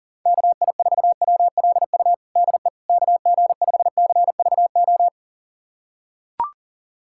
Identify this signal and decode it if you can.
Morse code, heard as a keyed tone, about 40 wpm, 700 Hz. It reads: KI4WLU DE KG5CVO
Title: USA Callsign DE Callsign - 6 Characters - 40wpm